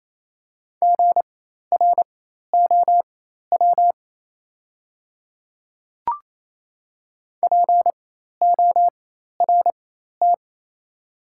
Morse code, heard as a keyed tone, about 28 wpm, spaced as for 13 wpm, 700 Hz. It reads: GROW PORT
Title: 4x Character Spacing - Top 300 Words - 28wpm